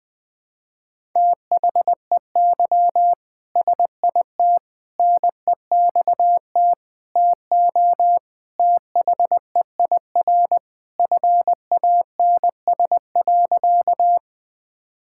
Morse code, heard as a keyed tone, about 21 wpm, 700 Hz. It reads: THEY SIT NEXT TO THEIR FANS.
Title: All-Age Appropriate Jokes - 20wpm